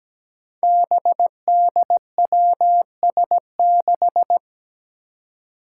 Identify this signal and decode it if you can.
Morse code, heard as a keyed tone, about 17 wpm, 700 Hz. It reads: BDWS6
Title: Five Challenging Letter-Number Combinations 17wpm